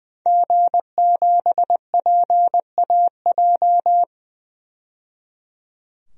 Morse code, heard as a keyed tone, about 20 wpm, 700 Hz. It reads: G7PAJ
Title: DXCC-100 Call Signs - 5 Characters - 20wpm